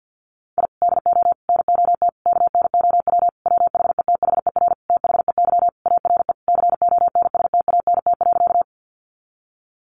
Morse code, heard as a keyed tone, about 48 wpm, 700 Hz. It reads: I DO NOT KNOW WHETHER THEY ARE CONSTANT.
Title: Sentences from Top 1500 Words 50wpm